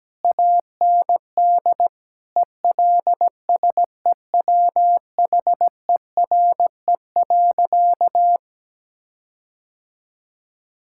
Morse code, heard as a keyed tone, about 17 wpm, 700 Hz. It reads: AND ELSEWHERE.
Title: Sentences from Top 1600 Words 17wpm